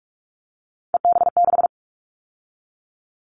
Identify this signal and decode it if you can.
Morse code, heard as a keyed tone, about 46 wpm, 700 Hz. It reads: EB6
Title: Callsign Prefixes for Top 50 DXCC - 45wpm